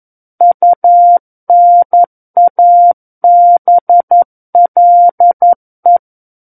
Morse code, heard as a keyed tone, about 11 wpm, 700 Hz. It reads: UNABLE
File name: 11 WPM